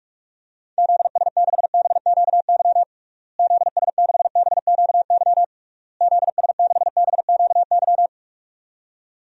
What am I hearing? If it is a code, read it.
Morse code, heard as a keyed tone, about 46 wpm, 700 Hz. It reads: ZS6BQY ZS6BQY ZS6BQY
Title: DXCC-100 Call Signs - 6 Characters - Repeated 3x - 45wpm